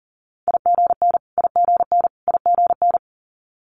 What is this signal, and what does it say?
Morse code, heard as a keyed tone, about 40 wpm, 700 Hz. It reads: IGN IGN IGN
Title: Multi-letter Phonemes Repeated 3x 40wpm